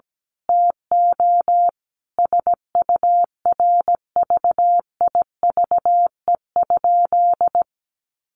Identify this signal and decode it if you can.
Morse code, heard as a keyed tone, about 17 wpm, 700 Hz. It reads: TO SURVIVE?
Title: A Sound Of Thunder 17wpm